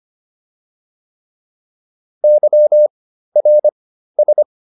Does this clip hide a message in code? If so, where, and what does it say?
Yes: Morse code from 2.2 to 4.4 s, heard as a keyed tone, about 25 wpm, 600 Hz. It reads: Y R S